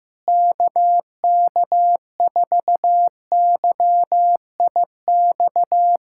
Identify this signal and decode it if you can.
Morse code, heard as a keyed tone, about 15 wpm, 700 Hz. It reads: KK4YIX
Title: USA Callsign DE Callsign - 6 Characters - 15wpm